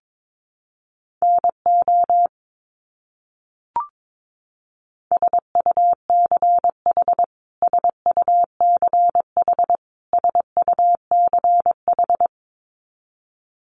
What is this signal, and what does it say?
Morse code, heard as a keyed tone, about 22 wpm, 700 Hz. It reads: NO SUCH SUCH SUCH
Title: Top 200 Words Repeated 22wpm